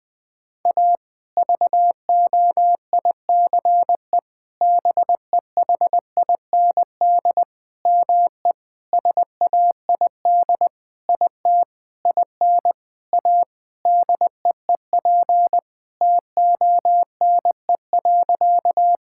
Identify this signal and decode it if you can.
Morse code, heard as a keyed tone, about 21 wpm, 700 Hz. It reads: A VOICE BEHIND ME SAID IT IN A DEEP TONE.
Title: Sentences from Top 1600 Words 20wpm